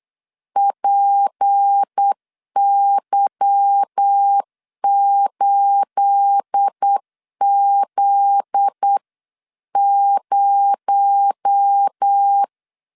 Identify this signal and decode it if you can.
Morse code, heard as a keyed tone, about 8 wpm, 800 Hz. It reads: PY8Z0
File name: CW / DET / 1 9 1500C S C 8wpm 5653